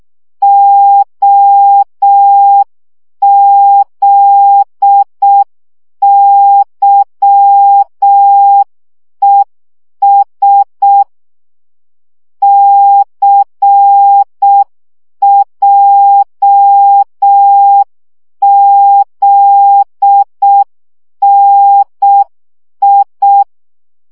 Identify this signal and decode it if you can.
Morse code, heard as a keyed tone, about 6 wpm, 800 Hz. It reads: OZYES CJZNI
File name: morze vizsga 2011-1 5